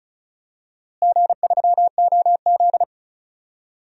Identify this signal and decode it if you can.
Morse code, heard as a keyed tone, about 34 wpm, 700 Hz. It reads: G3OZ
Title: DXCC-100 Call Signs - 4 Characters - Repeated 2x - 35wpm